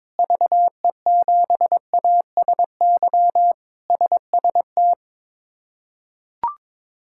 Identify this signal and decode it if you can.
Morse code, heard as a keyed tone, about 22 wpm, 700 Hz. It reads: VE7ASY SST
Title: K1USN SST QSO Practice - 22wpm